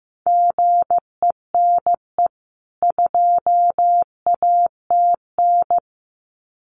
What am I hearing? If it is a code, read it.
Morse code, heard as a keyed tone, about 15 wpm, 700 Hz. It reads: GENE 2ATN
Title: CWT Exchange 15wpm